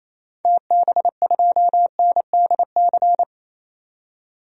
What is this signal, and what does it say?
Morse code, heard as a keyed tone, about 28 wpm, 700 Hz. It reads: TB2NDC
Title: DXCC-100 Call Signs - 6 Characters - 28wpm